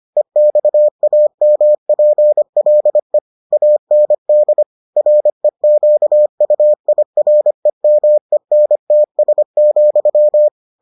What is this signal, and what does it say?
Morse code, heard as a keyed tone, about 25 wpm, 600 Hz. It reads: EXAMPLE AND REQUIREMENTS,